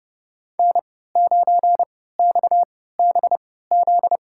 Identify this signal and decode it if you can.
Morse code, heard as a keyed tone, about 30 wpm, spaced as for 17 wpm, 700 Hz. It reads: N9XBZ
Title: USA Call Signs - 3x Character Spacing - 30wpm